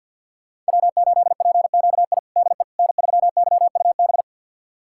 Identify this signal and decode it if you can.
Morse code, heard as a keyed tone, about 52 wpm, 700 Hz. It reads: W8PQI DE N3YUB
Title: USA Callsign DE Callsign - 5 Characters - 50wpm